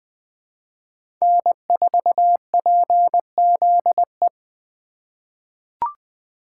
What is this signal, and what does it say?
Morse code, heard as a keyed tone, about 20 wpm, 700 Hz. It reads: N4PZE
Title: USA Call Signs - 5 Characters - 2x Character Spacing - 20wpm